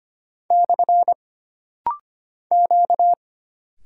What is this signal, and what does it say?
Morse code, heard as a keyed tone, about 25 wpm, 700 Hz. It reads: / Q
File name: Single Character (Letter:Number:Punctuation:Prosign) - Rapid-Fire - 25wpm